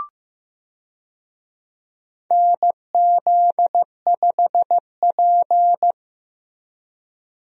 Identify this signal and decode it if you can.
Morse code, heard as a keyed tone, about 15 wpm, 700 Hz. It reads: NZ5P
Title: USA Call Signs - 4 Characters - Repeated 2x - 15wpm